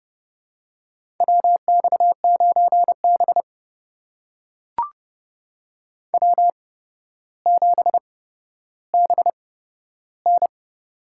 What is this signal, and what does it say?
Morse code, heard as a keyed tone, about 30 wpm, spaced as for 13 wpm, 700 Hz. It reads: WX9B W7BN
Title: USA Call Signs - 4 Characters - 8x Character Spacing - 30wpm